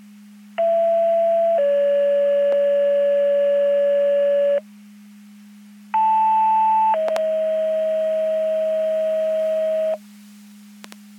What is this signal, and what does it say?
Two-tone page, heard as a one-second tone, then a three-second tone, about 670 then 570 Hz, about 880 then 640 Hz. It is heard from 0.6 to 9.9 s.
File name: Portage 154.385 Paging